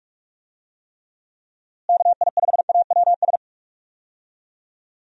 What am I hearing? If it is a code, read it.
Morse code, heard as a keyed tone, about 43 wpm, 700 Hz. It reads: KI5AWS
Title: USA Call Signs - 6 Characters - Repeated 2x - 45wpm